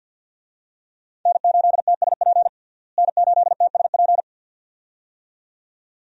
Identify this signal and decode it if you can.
Morse code, heard as a keyed tone, about 47 wpm, 700 Hz. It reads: N8TSP N8TSP
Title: USA Call Signs - 5 Characters - Repeated 2x - 50wpm